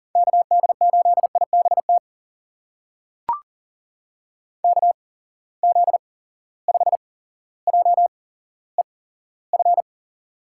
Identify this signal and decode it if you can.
Morse code, heard as a keyed tone, about 40 wpm, spaced as for 17 wpm, 700 Hz. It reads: KD8IBT KZ5JEF
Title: USA Call Signs - 6 Characters - 8x Character Spacing - 40wpm